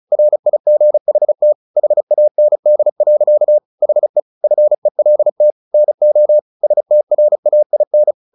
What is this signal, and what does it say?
Morse code, heard as a keyed tone, about 34 wpm, 600 Hz. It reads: RIGHT HAND. HE FELT NO STRAIN
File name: the old man and the sea / the old man and the sea 1 35wpm